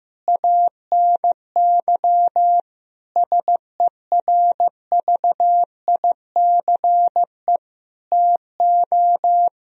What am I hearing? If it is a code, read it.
Morse code, heard as a keyed tone, about 15 wpm, 700 Hz. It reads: ANY SERVICE TO
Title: Sentences from Top 1000 Words - Review Part 4 - 15wpm